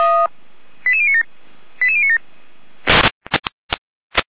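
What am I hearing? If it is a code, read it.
Five-tone selective call (ZVEI), heard as quick runs of five tones, tones of about 70 ms. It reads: 88097 88097
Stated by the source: ZVEI five-tone selective call received on an SDR.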